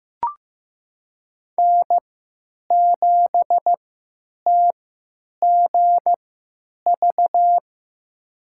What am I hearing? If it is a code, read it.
Morse code, heard as a keyed tone, about 15 wpm, spaced as for 9 wpm, 700 Hz. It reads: N7TGV
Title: USA Call Signs - 3x Character Spacing - 15wpm